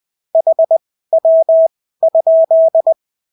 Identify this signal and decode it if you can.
Morse code, heard as a keyed tone, about 20 wpm, spaced as for 14 wpm, 650 Hz. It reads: HW?
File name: CW QSO libroCW4 20-10ppm